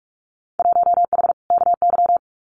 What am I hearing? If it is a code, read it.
Morse code, heard as a keyed tone, about 44 wpm, 700 Hz. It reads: 1H KY
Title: Winter Field Day Exchanges 45wpm